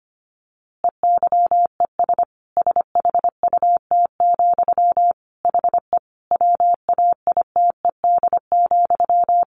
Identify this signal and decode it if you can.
Morse code, heard as a keyed tone, about 25 wpm, 700 Hz. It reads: EYES SHUT, HE WAITED,
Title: A Sound Of Thunder 25wpm